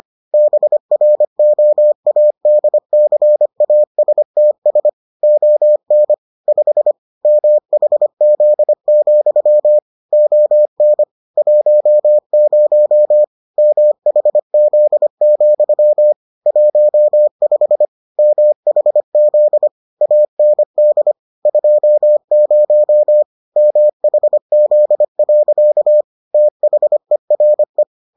Morse code, heard as a keyed tone, about 25 wpm, 600 Hz. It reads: BROADCASTS ON 5 MHZ, ON 10 MHZ, 15 MHZ AND 20 MHZ. THERE